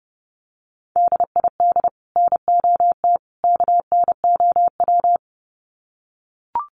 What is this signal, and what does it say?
Morse code, heard as a keyed tone, about 30 wpm, 700 Hz. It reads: DID NOT KNOW
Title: Sets of 3 Words 30wpm